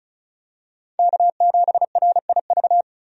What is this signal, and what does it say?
Morse code, heard as a keyed tone, about 34 wpm, 700 Hz. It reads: K7RIV
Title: USA Call Signs - 5 Characters - 35wpm